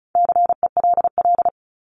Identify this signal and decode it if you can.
Morse code, heard as a keyed tone, about 34 wpm, 700 Hz. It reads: CELL
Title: Top 1001-1250 Words 35wpm